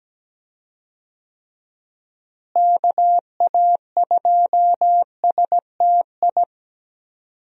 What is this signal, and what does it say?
Morse code, heard as a keyed tone, about 17 wpm, 700 Hz. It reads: KA2STI
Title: USA Call Signs - 6 Characters - Repeated 2x - 17wpm